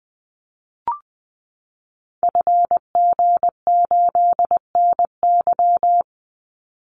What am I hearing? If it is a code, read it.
Morse code, heard as a keyed tone, about 20 wpm, 700 Hz. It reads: FG8NY
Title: Call Signs Repeated 2x - 20wpm